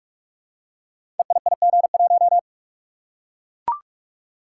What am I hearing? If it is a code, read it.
Morse code, heard as a keyed tone, about 44 wpm, 700 Hz. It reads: EIIG1
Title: Code Groups - 5 Letter:Numbers - 45wpm